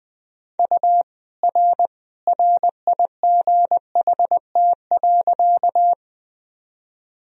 Morse code, heard as a keyed tone, about 20 wpm, 700 Hz. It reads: U R RIGHT.
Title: Sentences from Top 200 Words and Abbreviations 20wpm